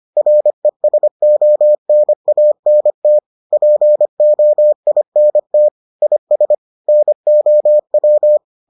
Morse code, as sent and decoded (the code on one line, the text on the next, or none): .-. ... --- -. .- -. - / .--. --- .. -. - / .. ... / -. --- .--
RESONANT POINT IS NOW